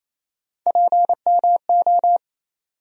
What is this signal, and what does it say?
Morse code, heard as a keyed tone, about 28 wpm, 700 Hz. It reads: PMO
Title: USA Call Sign Postfixes - 3 Letters 28wpm